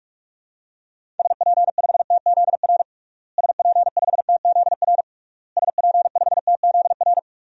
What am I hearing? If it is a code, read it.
Morse code, heard as a keyed tone, about 41 wpm, 700 Hz. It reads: SP5TZR SP5TZR SP5TZR
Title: DXCC-100 Call Signs - 6 Characters - Repeated 3x - 45wpm